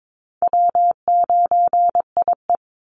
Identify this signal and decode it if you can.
Morse code, heard as a keyed tone, about 22 wpm, 700 Hz. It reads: W9IE
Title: Easy Call Signs Repeated 3x - 22wpm